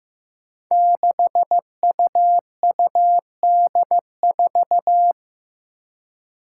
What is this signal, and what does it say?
Morse code, heard as a keyed tone, about 15 wpm, 700 Hz. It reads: 6UUD4
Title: Five Challenging Letter-Number Combinations 15wpm